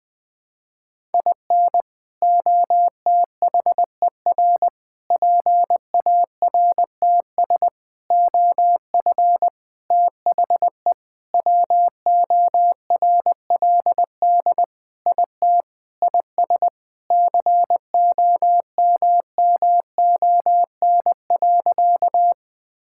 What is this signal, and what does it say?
Morse code, heard as a keyed tone, about 21 wpm, 700 Hz. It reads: IN OTHER PARTS OF THE WORLD IT IS COMMON.
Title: Sentences with 6-Word Phrases - 20wpm